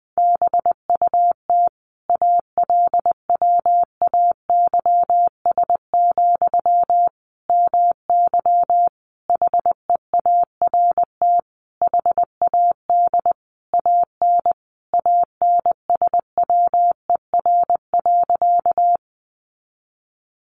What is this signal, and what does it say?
Morse code, heard as a keyed tone, about 20 wpm, 700 Hz. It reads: BUT ALWAYS, MY HEART HAD AN ANSWER.
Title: Sentences from Top 1200 Words 20wpm